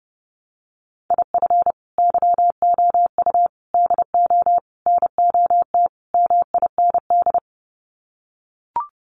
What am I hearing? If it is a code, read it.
Morse code, heard as a keyed tone, about 30 wpm, 700 Hz. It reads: IF YOU DO NOT MIND
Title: Sets of 5 Words 30wpm